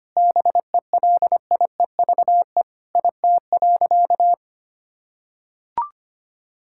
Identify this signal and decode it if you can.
Morse code, heard as a keyed tone, about 25 wpm, 700 Hz. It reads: BELIEVE IT.
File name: practice / 25wpm / Sentences with 3-Word Phrases - 25wpm